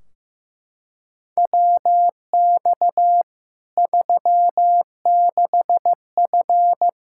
Morse code, heard as a keyed tone, about 15 wpm, 700 Hz. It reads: WX 36F
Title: Practice QSO - Exchange 2 - Info - 15wpm